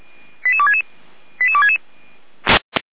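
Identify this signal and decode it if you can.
Five-tone selective call (ZVEI), heard as quick runs of five tones, tones of about 70 ms. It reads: 88266 88266
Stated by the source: Five-tone ZVEI selective call received on an SDR.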